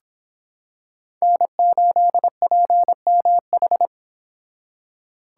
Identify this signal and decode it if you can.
Morse code, heard as a keyed tone, about 25 wpm, 700 Hz. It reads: N8PMH